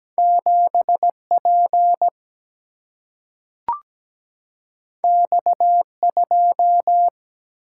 Morse code, heard as a keyed tone, about 17 wpm, 700 Hz. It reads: 7P X2
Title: Single-Letter Single-Number 17wpm